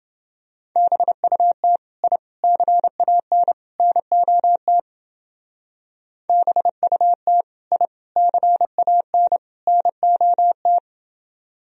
Morse code, heard as a keyed tone, about 29 wpm, 700 Hz. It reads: BUT I CAN NOT BUT I CAN NOT